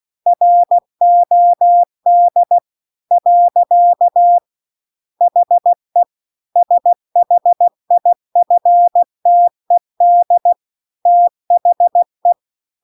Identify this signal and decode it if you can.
Morse code, heard as a keyed tone, about 16 wpm, 700 Hz. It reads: ROD . HE SHIFTED THE